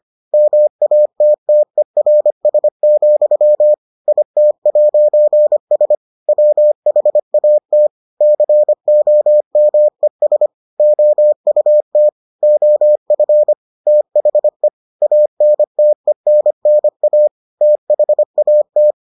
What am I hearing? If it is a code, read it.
Morse code, heard as a keyed tone, about 25 wpm, 600 Hz. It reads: MATTERS, IT'S WHAT COMES OUT OF THE ANTENNA THAT